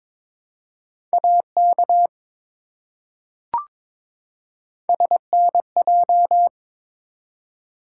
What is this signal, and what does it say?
Morse code, heard as a keyed tone, about 22 wpm, 700 Hz. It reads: AK SNJ
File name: practice / 22wpm / ARRL Sections v2 - 22wpm